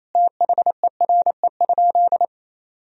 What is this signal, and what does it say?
Morse code, heard as a keyed tone, about 28 wpm, 700 Hz. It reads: THERE?
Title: Sentences from Top 100 Words and Abbreviations 28wpm